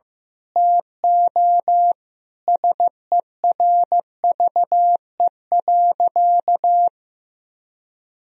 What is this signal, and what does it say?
Morse code, heard as a keyed tone, about 15 wpm, 700 Hz. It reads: TO SERVE.
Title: Sentences from Top 1000 Words - Review Part 4 - 15wpm